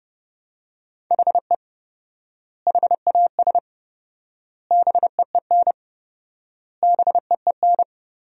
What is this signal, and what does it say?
Morse code, heard as a keyed tone, about 32 wpm, 700 Hz. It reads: HE HAS BEEN BEEN
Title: 4x Word-Spacing - Sentences from Top 300 Words - 30wpm